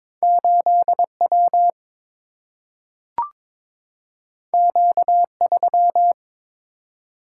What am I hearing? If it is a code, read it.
Morse code, heard as a keyed tone, about 22 wpm, 700 Hz. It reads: 8W Q3
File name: practice / 22wpm / Two Letter-Number Combinations - 22wpm